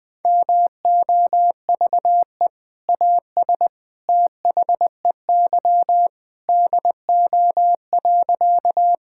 Morse code, heard as a keyed tone, about 20 wpm, 700 Hz. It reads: MOVE AS THEY DO.
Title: Sentences with 2-Word Phrases - Spoken Beginning - 20wpm